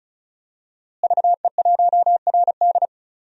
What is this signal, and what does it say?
Morse code, heard as a keyed tone, about 36 wpm, 700 Hz. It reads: VE1RD
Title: DXCC-100 Call Signs - 5 Characters - Repeated 3x - 35wpm